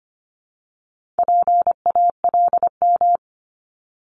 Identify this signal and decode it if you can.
Morse code, heard as a keyed tone, about 25 wpm, 700 Hz. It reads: PALM
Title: Multi-letter Phonemes 25wpm